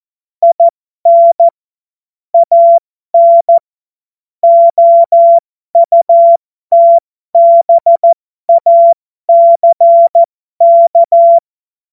Morse code, heard as a keyed tone, about 14 wpm, spaced as for 12 wpm, 680 Hz. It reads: IN AN OUTBACK
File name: Morsekurs / Witze